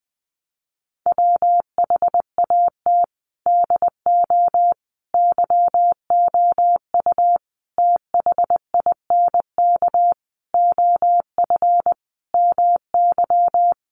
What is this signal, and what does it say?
Morse code, heard as a keyed tone, about 20 wpm, 700 Hz. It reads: WHAT DO YOU THINK OF MY
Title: Sentences from Top 1200 Words 20wpm